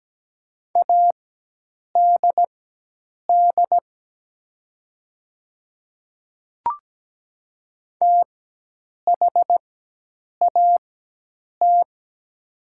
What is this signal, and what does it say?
Morse code, heard as a keyed tone, about 17 wpm, spaced as for 8 wpm, 700 Hz. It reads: ADD THAT
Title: 4x Character Spacing - Top 200 Words - 17wpm